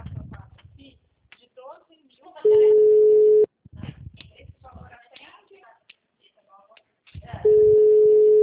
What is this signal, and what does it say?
Ringback tone (Europe, 425 Hz), with speech.